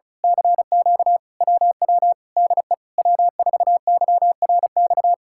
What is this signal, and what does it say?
Morse code, heard as a keyed tone, about 33 wpm, 700 Hz. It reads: CQ WW DE W4YRX
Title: Speed-Racing Common QSO Phrases - 35wpm